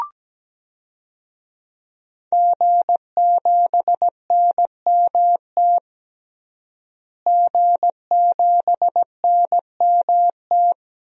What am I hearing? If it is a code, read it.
Morse code, heard as a keyed tone, about 17 wpm, 700 Hz. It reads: G7NMT G7NMT
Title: DXCC-100 Call Signs - 5 Characters - Repeated 3x - 17wpm